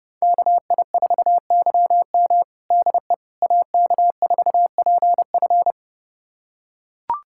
Morse code, heard as a keyed tone, about 30 wpm, 700 Hz. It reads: KI4YM DE AK4PF
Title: USA Callsign DE Callsign - 5 Characters - 30wpm